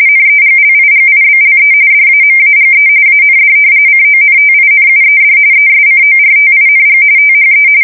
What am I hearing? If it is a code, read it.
RTTY, about 45.45 baud. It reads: NEGLIGIBLE DAMAGE AND CONTINUED ESCORTING THE
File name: fsk170 / battleshipnj